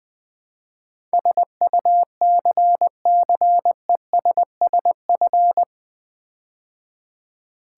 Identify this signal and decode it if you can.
Morse code, heard as a keyed tone, about 20 wpm, 700 Hz. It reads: SUCCESSF
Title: Build A Word - 4 Letters At A Time - 12-Letter Words - 20wpm